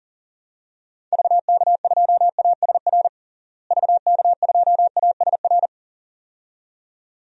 Morse code, heard as a keyed tone, about 40 wpm, 700 Hz. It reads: VK2ASR VK2ASR
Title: DXCC-100 Call Signs - 6 Characters - Repeated 2x - 40wpm